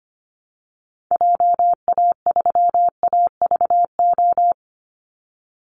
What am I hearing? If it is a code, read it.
Morse code, heard as a keyed tone, about 25 wpm, 700 Hz. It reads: JA3AVO
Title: Call Signs - 25wpm